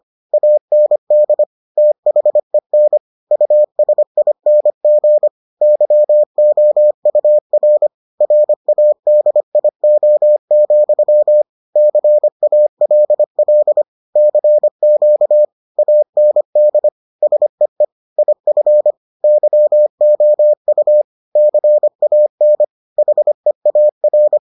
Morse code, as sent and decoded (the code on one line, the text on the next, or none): .- -. -.. / - .... -. / ..- ... .. -. --. / -.-- --- ..- .-. / .-. .- -.. .. --- --..-- / -.-. .- .-.. .-.. / -.-. --.- / .- -. -.. / ... . / .. ..-. / -.-- --- ..- / -.-. .- -. / .... .- .-.
AND THEN USING YOUR RADIO, CALL CQ AND SEE IF YOU CAN HEAR